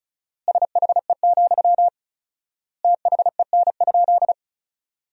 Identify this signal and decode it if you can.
Morse code, heard as a keyed tone, about 36 wpm, 700 Hz. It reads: SHE, THEN?
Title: 4x Word-Spacing - Sentences from Top 200 Words - 35wpm